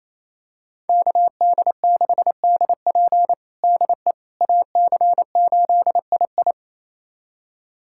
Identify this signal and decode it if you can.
Morse code, heard as a keyed tone, about 28 wpm, 700 Hz. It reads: KD6DP DE AC8II
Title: USA Callsign DE Callsign - 5 Characters - 28wpm